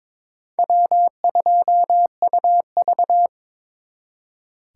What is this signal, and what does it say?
Morse code, heard as a keyed tone, about 26 wpm, 700 Hz. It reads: W2UV
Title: Speed Racing - USA 1x2 Call Signs 22wpm